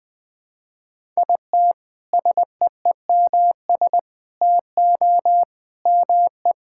Morse code, heard as a keyed tone, about 20 wpm, 700 Hz. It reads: IT SEEMS TO ME
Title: Sentences with 4-Word Phrases - 20wpm